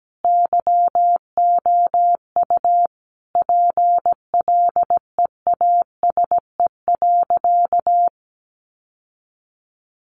Morse code, heard as a keyed tone, about 17 wpm, 700 Hz. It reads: YOU PLEASE.
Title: Sentences from Top 1500 Words 17wpm